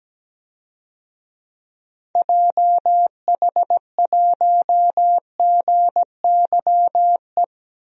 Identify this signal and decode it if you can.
Morse code, heard as a keyed tone, about 17 wpm, 700 Hz. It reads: JH1GYE
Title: DXCC-100 Call Signs - 6 Characters - 17wpm